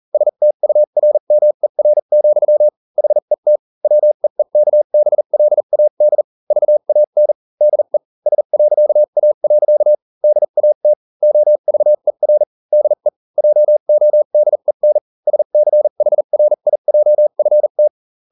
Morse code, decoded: STURMER, HET WEEKBLAD VAN DE S.A. DAT OVER DE JODEN SCHRIJFT